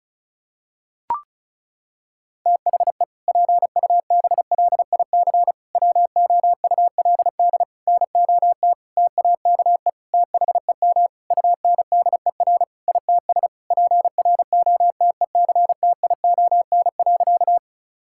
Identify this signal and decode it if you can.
Morse code, heard as a keyed tone, about 36 wpm, 700 Hz. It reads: THE PUBLIC WOULD NOT TAKE THEM UNDER ITS PROTECTION.
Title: Sentences from Top 1600 Words 35wpm